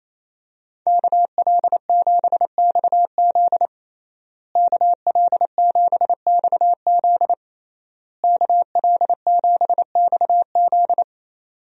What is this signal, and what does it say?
Morse code, heard as a keyed tone, about 28 wpm, 700 Hz. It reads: KL7XZ KL7XZ KL7XZ
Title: DXCC-100 Call Signs - 5 Characters - Repeated 3x - 28wpm